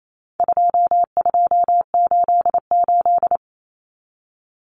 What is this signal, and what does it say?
Morse code, heard as a keyed tone, about 28 wpm, 700 Hz. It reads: 2288
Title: Common Cut Numbers - 4-Digit 28wpm